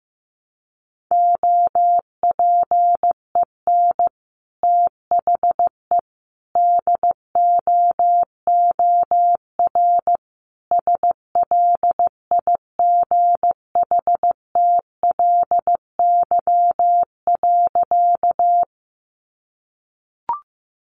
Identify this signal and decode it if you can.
Morse code, heard as a keyed tone, about 15 wpm, 700 Hz. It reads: OPEN THE DOOR SLIGHTLY.
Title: Sentences from Top 1300 Words 15wpm